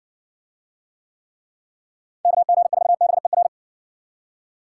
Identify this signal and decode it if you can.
Morse code, heard as a keyed tone, about 60 wpm, 700 Hz. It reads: KG4BER